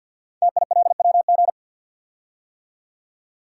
Morse code, heard as a keyed tone, about 48 wpm, 700 Hz. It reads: TILWG
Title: Code Groups - 5 Letters - 50wpm